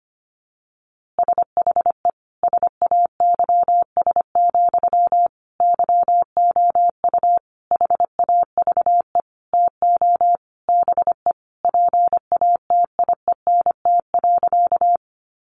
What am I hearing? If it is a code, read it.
Morse code, heard as a keyed tone, about 25 wpm, 700 Hz. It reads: SHE SAYS, YOU HAVE TO BE PATIENT.
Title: Easy Phrases 25wpm